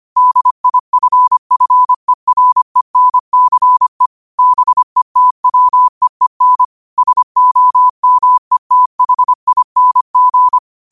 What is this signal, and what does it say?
Morse code, heard as a keyed tone, about 25 wpm, 1000 Hz. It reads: DIFFERENCE BETWEEN SOMETHING
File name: qotdfiles / 25wpm